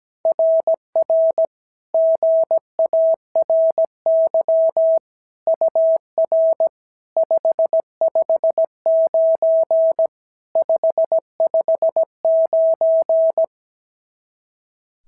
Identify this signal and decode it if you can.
Morse code, heard as a keyed tone, about 16 wpm, 650 Hz. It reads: RR GARY UR 559 559
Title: SOTA QSO Practice 17wpm